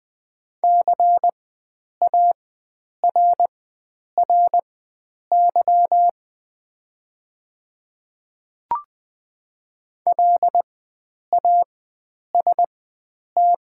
Morse code, heard as a keyed tone, about 20 wpm, spaced as for 9 wpm, 700 Hz. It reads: CARRY LAST
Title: 4x Character Spacing - Top 300 Words - 20wpm